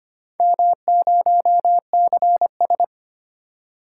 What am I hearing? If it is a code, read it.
Morse code, heard as a keyed tone, about 25 wpm, 700 Hz. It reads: M0CS
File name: practice / 25wpm / Four Letter-Number Combinations - 25wpm